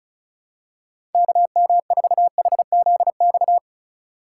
Morse code, heard as a keyed tone, about 36 wpm, 700 Hz. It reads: KM4HZX
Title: USA Call Signs - 6 Characters - 4x Character Spacing - 35wpm